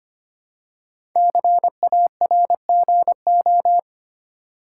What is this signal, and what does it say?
Morse code, heard as a keyed tone, about 25 wpm, 700 Hz. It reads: CARGO
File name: 5 Letter Words - First 3 Letters Given - 25wpm